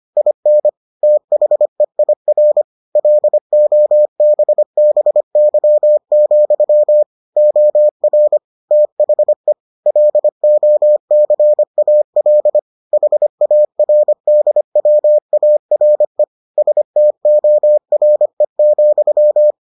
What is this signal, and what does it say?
Morse code, heard as a keyed tone, about 25 wpm, 600 Hz. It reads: IN THEIR LOBBY, OR THE LOCAL HARDWARE STORE,